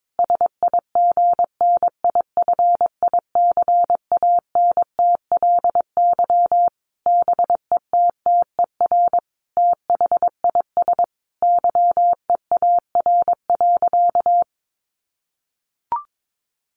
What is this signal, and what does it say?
Morse code, heard as a keyed tone, about 22 wpm, 700 Hz. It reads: SIGNIFICANTLY BETTER THIS YEAR.
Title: Sentences from Top 1500 Words 22wpm